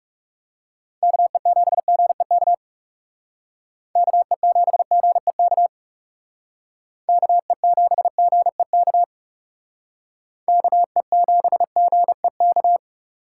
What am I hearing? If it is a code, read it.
Morse code, heard as a keyed tone, about 36 wpm, 700 Hz. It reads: KE7GEK KE7GEK KE7GEK KE7GEK
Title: Speed Racing - USA Call Signs 30wpm